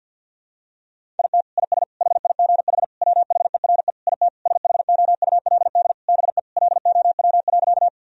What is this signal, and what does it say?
Morse code, heard as a keyed tone, about 52 wpm, 700 Hz. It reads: IT IS HIGH WHERE IT SHOULD BE LOW.
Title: Sentences from Top 1000 Words - Review Part 1 - 50wpm